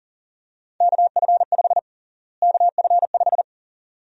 Morse code, heard as a keyed tone, about 39 wpm, 700 Hz. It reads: KF5 KF5
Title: USA Call Sign Prefixes - Repeated 2x - 40wpm